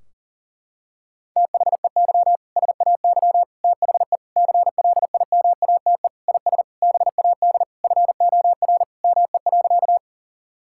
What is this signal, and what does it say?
Morse code, heard as a keyed tone, about 40 wpm, 700 Hz. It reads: THEY SAY THE CLIMATE IS BAD FOR ME.
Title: Sentences from Top 1600 Words 40wpm